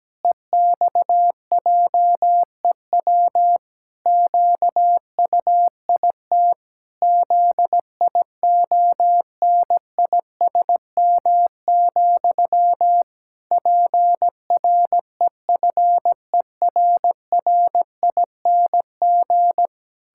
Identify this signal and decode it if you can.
Morse code, heard as a keyed tone, about 17 wpm, 700 Hz. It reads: EXJEW QUIT ZIONISM, PREFERRING
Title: Pangram Challenge v2 - 17wpm